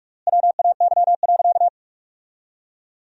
Morse code, heard as a keyed tone, about 45 wpm, 700 Hz. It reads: WAY.
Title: 4x Word-Spacing - Sentences from Top 100 Words - 45wpm